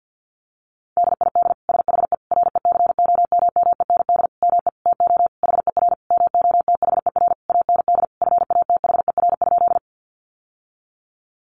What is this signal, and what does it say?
Morse code, heard as a keyed tone, about 50 wpm, 700 Hz. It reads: DID SHE RECOMMEND ME TO HER MOTHER AND FATHER?
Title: Sentences from Top 1200 Words 50wpm